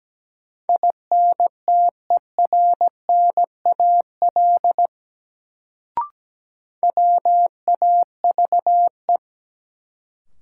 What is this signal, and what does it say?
Morse code, heard as a keyed tone, about 17 wpm, 700 Hz. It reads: INTERNAL WAVE
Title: Sentences from Top 1400 Words - New Word Drill Alt - 17wpm